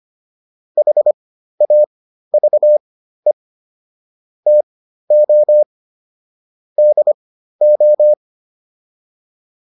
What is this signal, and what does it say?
Morse code, heard as a keyed tone, about 25 wpm, spaced as for 13 wpm, 600 Hz. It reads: HAVE TO DO